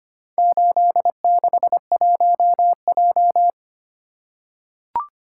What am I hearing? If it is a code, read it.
Morse code, heard as a keyed tone, about 25 wpm, 700 Hz. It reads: 861J
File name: practice / 25wpm / Four Letter-Number Combinations - 25wpm